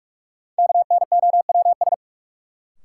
Morse code, heard as a keyed tone, about 45 wpm, 700 Hz. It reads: KNOWS
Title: Sentences from Top 1000 Words - Review Part 1 - New Word Drill - 45wpm